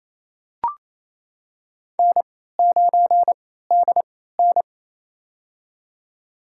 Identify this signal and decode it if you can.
Morse code, heard as a keyed tone, about 28 wpm, spaced as for 16 wpm, 700 Hz. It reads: N9DN
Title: USA Call Signs - 4 Characters - 3x Character Spacing - 28wpm